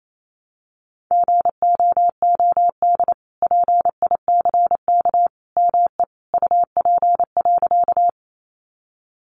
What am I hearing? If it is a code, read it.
Morse code, heard as a keyed tone, about 28 wpm, 700 Hz. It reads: GOOD PICK ME UP.
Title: Sentences from Top 1000 Words 28wpm